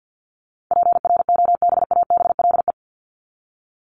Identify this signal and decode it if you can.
Morse code, heard as a keyed tone, about 48 wpm, 700 Hz. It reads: PROBABLE
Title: LY Suffix - Top 100 Words with Base Words 50wpm